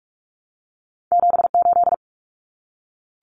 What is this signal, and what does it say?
Morse code, heard as a keyed tone, about 46 wpm, 700 Hz. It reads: 78
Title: Common Cut Numbers - 2-Digit 45wpm